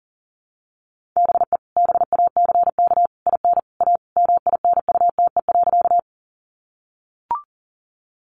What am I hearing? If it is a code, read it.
Morse code, heard as a keyed tone, about 40 wpm, 700 Hz. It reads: BE BACK IN A MINUTE.
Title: Sentences from Top 500 Words 40wpm